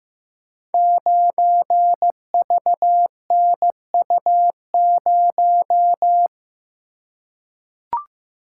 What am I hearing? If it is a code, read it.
Morse code, heard as a keyed tone, about 15 wpm, 700 Hz. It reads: 9VNU0
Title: Code Groups - 5 Letter:Numbers - 15wpm